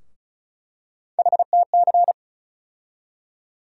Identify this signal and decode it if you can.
Morse code, heard as a keyed tone, about 36 wpm, 700 Hz. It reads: HTC
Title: USA Call Sign Postfixes - 3 Letters 35wpm